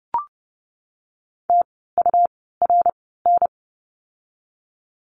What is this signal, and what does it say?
Morse code, heard as a keyed tone, about 30 wpm, spaced as for 17 wpm, 700 Hz. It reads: TURN
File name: course / 30:10wpm / 108 Lesson 18f - Letter m - Words 30wpm